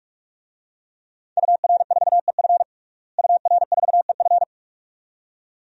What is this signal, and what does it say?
Morse code, heard as a keyed tone, about 46 wpm, 700 Hz. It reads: UR4EF UR4EF
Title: DXCC-100 Call Signs - 5 Characters - Repeated 2x - 45wpm